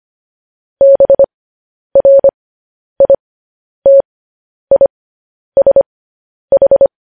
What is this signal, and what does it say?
Morse code, heard as a keyed tone, about 25 wpm, spaced as for 10 wpm, 550 Hz. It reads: BRITISH
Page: news headlines in morse code at 10 wpm